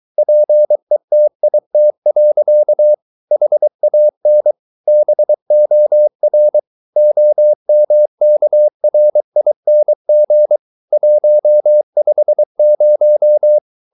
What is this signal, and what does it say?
Morse code, heard as a keyed tone, about 23 wpm, 600 Hz. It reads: PETIT. HAN BOR OMKRING 150
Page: Ferie 23 wpm